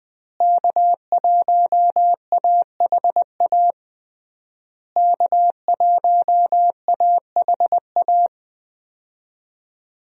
Morse code, heard as a keyed tone, about 21 wpm, 700 Hz. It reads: K1AHA K1AHA
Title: USA Call Signs - 5 Characters - Repeated 3x - 20wpm